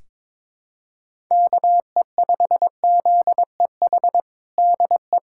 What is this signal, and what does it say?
Morse code, heard as a keyed tone, about 22 wpm, 700 Hz. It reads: KE5ZEH DE
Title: USA Callsign DE Callsign 22wpm